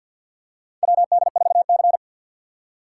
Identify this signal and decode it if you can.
Morse code, heard as a keyed tone, about 51 wpm, 700 Hz. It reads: WD4C
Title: USA Call Signs - 2x Character Spacing - 50wpm